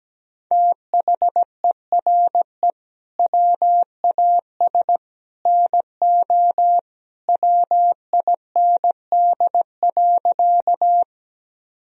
Morse code, heard as a keyed tone, about 17 wpm, 700 Hz. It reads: THERE WAS NO WIND.
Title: Sentences from Top 1000 Words - Review Part 2 - 17wpm